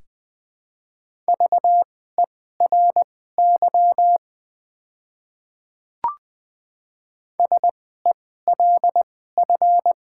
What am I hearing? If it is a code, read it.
Morse code, heard as a keyed tone, about 20 wpm, spaced as for 15 wpm, 700 Hz. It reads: VERY SELF
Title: 2x Character Spacing - Top 200 Words - 20wpm